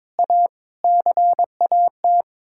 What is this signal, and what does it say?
Morse code, heard as a keyed tone, about 22 wpm, 700 Hz. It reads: A CAT
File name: Sets of 9 Words - 22wpm